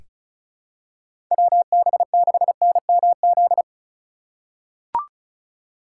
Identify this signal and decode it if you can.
Morse code, heard as a keyed tone, about 36 wpm, 700 Hz. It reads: WB6NMZ
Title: USA Call Signs - 6 Characters - 4x Character Spacing - 35wpm